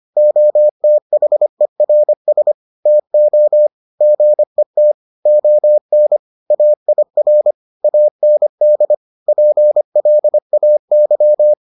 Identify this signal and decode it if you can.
Morse code, heard as a keyed tone, about 25 wpm, 600 Hz. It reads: OTHERS TO GET ON AIR AND PLAY